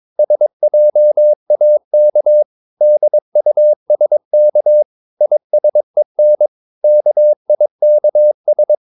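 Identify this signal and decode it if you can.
Morse code, heard as a keyed tone, about 22 wpm, 600 Hz. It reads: SJAK DUSK ISEN KIKS
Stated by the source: Morse practice of short Danish words at 22 wpm, from CW med Gnister.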